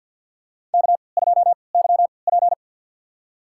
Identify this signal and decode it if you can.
Morse code, heard as a keyed tone, about 53 wpm, spaced as for 29 wpm, 700 Hz. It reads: K2YP
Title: USA Call Signs - 4 Characters - 3x Character Spacing - 50wpm